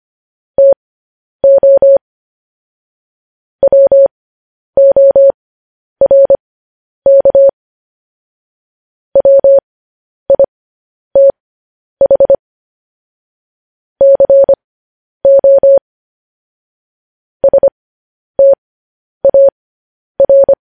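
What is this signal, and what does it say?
Morse code, heard as a keyed tone, about 25 wpm, spaced as for 10 wpm, 550 Hz. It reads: TO WORK WITH CO STAR
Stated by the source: Morse code transcription of news headlines.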